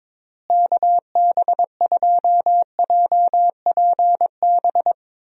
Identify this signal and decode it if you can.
Morse code, heard as a keyed tone, about 22 wpm, 700 Hz. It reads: KB2JPB
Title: USA Call Signs - 6 Characters - Repeated 3x - 22wpm